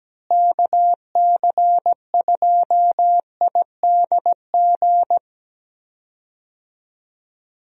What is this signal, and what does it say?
Morse code, heard as a keyed tone, about 17 wpm, 700 Hz. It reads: KC2IDG
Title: USA Call Signs - 6 Characters - Repeated 2x - 17wpm